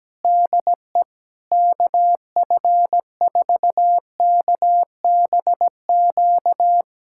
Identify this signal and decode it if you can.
Morse code, heard as a keyed tone, about 17 wpm, 700 Hz. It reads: DE KF4KBQ
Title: USA Callsign DE Callsign - 6 Characters - 17wpm